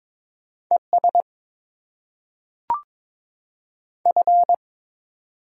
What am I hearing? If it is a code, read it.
Morse code, heard as a keyed tone, about 22 wpm, 700 Hz. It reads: ES F
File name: Top 20 QSO Elements - 6x Character Spacing - 22wpm